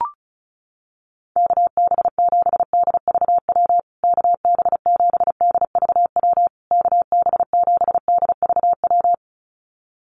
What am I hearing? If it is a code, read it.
Morse code, heard as a keyed tone, about 34 wpm, 700 Hz. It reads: KB7DVW KB7DVW KB7DVW
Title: USA Call Signs Repeated 3x 35wpm